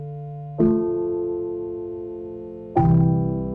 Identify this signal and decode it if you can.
Dial tone (North America), 350 + 440 Hz.